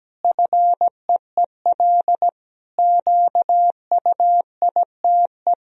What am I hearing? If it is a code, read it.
Morse code, heard as a keyed tone, about 17 wpm, 700 Hz. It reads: FEEL QUITE
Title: Sentences from Top 1000 Words - Review Part 1 - 17wpm